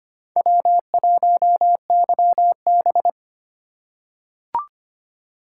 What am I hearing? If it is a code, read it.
Morse code, heard as a keyed tone, about 25 wpm, 700 Hz. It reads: W1YB
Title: USA Call Signs - 4 Characters - 4x Character Spacing - 25wpm